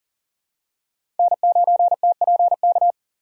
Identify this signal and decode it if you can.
Morse code, heard as a keyed tone, about 40 wpm, 700 Hz. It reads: N9TPK
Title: USA Call Signs - 2x Character Spacing - 40wpm